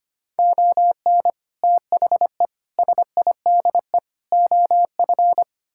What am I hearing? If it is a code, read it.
Morse code, heard as a keyed tone, about 25 wpm, 700 Hz. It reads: ON THE SIDE OF
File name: practice / 25wpm / Sentences with 3-Word Phrases - 25wpm